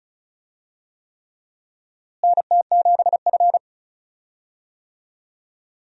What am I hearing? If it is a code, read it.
Morse code, heard as a keyed tone, about 36 wpm, 700 Hz. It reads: NT7F
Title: USA Call Signs - 4 Characters - Repeated 3x - 35wpm